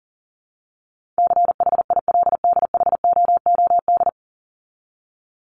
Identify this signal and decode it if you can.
Morse code, heard as a keyed tone, about 40 wpm, 700 Hz. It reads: CHILDHOOD
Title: Top 1751-2000 Words 40wpm